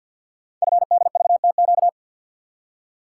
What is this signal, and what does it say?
Morse code, heard as a keyed tone, about 48 wpm, 700 Hz. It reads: FDVTQ
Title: Code Groups - 5 Letters - 50wpm